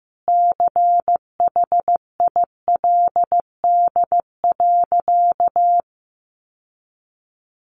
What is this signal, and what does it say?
Morse code, heard as a keyed tone, about 15 wpm, 700 Hz. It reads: CHILD.
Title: Sentences from Top 800 Words 15wpm